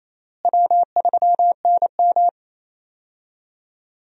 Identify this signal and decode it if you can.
Morse code, heard as a keyed tone, about 28 wpm, 700 Hz. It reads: W3NM
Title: USA Call Signs - 4 Characters - Repeated 3x - 28wpm